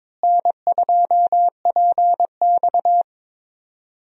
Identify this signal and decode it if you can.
Morse code, heard as a keyed tone, about 22 wpm, 700 Hz. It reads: N2PX
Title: USA Call Signs - 4 Characters - Repeated 3x - 22wpm